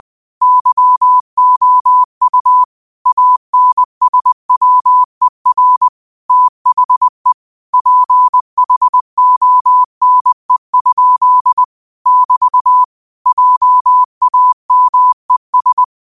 Morse code, heard as a keyed tone, about 20 wpm, 1000 Hz. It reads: YOU ANSWER THE PHONE? <BT> JAMES
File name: qotdfiles / 20wpm